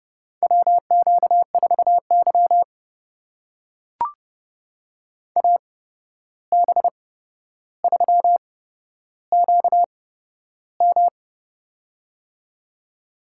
Morse code, heard as a keyed tone, about 31 wpm, spaced as for 12 wpm, 700 Hz. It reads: WQ4Y AB3QM
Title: USA Call Signs - 8x Character Spacing - 30wpm